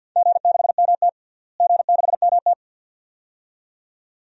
Morse code, heard as a keyed tone, about 50 wpm, 700 Hz. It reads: G6MT G6MT
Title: DXCC-100 Call Signs - 4 Characters - Repeated 2x - 50wpm